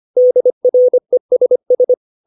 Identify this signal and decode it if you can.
Morse code, heard as a keyed tone, about 25 wpm, 500 Hz. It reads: DRESS
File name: 632 words / 632 Words 25WPM